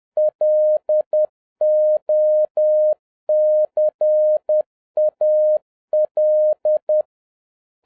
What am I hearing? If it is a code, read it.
Morse code, heard as a keyed tone, about 10 wpm, 610 Hz. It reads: LOCAL
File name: 2011-07-mp3 / n46 JOHN K3TN 10wpm